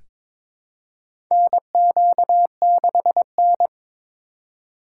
Morse code, heard as a keyed tone, about 23 wpm, 700 Hz. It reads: NQ6N
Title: USA Call Signs - 4 Characters - 4x Character Spacing - 22wpm